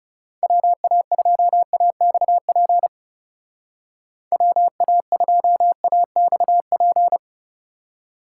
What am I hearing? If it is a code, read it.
Morse code, heard as a keyed tone, about 37 wpm, 700 Hz. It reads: WA2AXP WA2AXP
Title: Speed Racing - USA Call Signs 30wpm